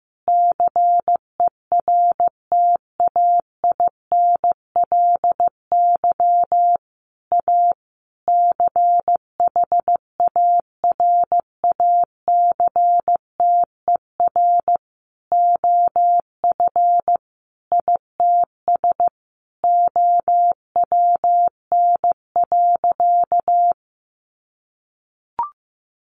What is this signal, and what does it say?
Morse code, heard as a keyed tone, about 15 wpm, 700 Hz. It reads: CERTAINLY A CHARACTER OF ITS OWN.
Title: Sentences from Top 1200 Words 15wpm